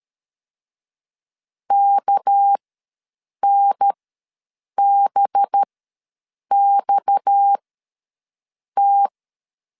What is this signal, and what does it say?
Morse code, heard as a keyed tone, about 13 wpm, spaced as for 7 wpm, 800 Hz. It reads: KNBXT
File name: LEZIONI / 6 PKX0 / 6 PKX ws 12wpm 3106